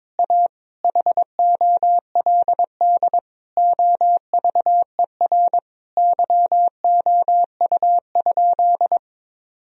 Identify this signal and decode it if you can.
Morse code, heard as a keyed tone, about 22 wpm, 700 Hz. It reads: A HOLD OVER YOU?
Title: Sentences from Top 1000 Words - Review Part 2 - 22wpm